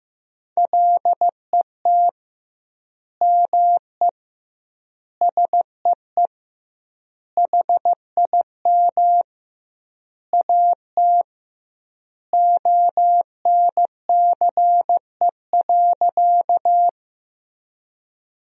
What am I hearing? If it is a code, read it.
Morse code, heard as a keyed tone, about 15 wpm, 700 Hz. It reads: LET ME SEE HIM AT ONCE.
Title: 2x Word-Spacing - Sentences from Top 300 Words - 15wpm